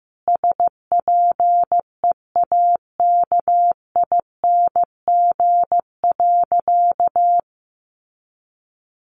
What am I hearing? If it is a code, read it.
Morse code, heard as a keyed tone, about 15 wpm, 700 Hz. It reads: SPEAKING.
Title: A Sound Of Thunder 15wpm